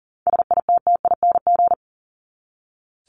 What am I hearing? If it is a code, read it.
Morse code, heard as a keyed tone, about 40 wpm, 700 Hz. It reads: SITTING
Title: ING Suffix - Top 100 Words with Base Words 40wpm